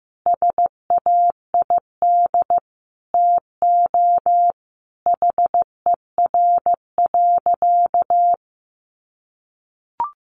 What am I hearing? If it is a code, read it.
Morse code, heard as a keyed tone, about 15 wpm, 700 Hz. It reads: SAID TO HER.
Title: Sentences from Top 1500 Words 15wpm